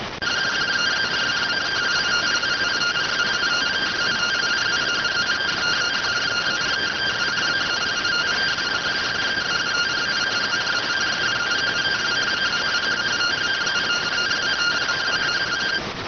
Two-tone teleprinter signal, about 170 Hz shift.